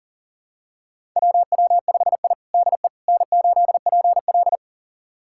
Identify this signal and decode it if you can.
Morse code, heard as a keyed tone, about 40 wpm, 700 Hz. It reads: WW5I DE N8PL
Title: USA Callsign DE Callsign - 4 Characters - 40wpm